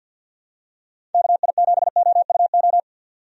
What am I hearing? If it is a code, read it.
Morse code, heard as a keyed tone, about 48 wpm, 700 Hz. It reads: KI7OUO
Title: USA Call Signs - 6 Characters - 50wpm